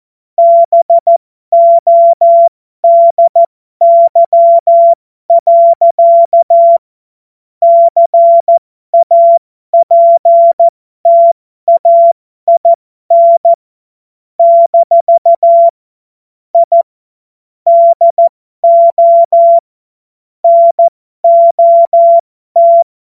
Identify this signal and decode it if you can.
Morse code, heard as a keyed tone, about 14 wpm, spaced as for 12 wpm, 680 Hz. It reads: BODY. CAPTAIN - I DO NOT